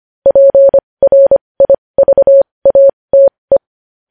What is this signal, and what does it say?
Morse code, heard as a keyed tone, about 25 wpm, spaced as for 20 wpm, 550 Hz. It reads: PRIVATE
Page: news-headlines-in-morse-code-at-20-wpm